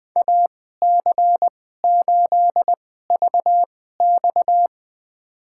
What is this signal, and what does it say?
Morse code, heard as a keyed tone, about 21 wpm, spaced as for 15 wpm, 700 Hz. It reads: AC8VX
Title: USA Call Signs - 5 Characters - 2x Character Spacing - 20wpm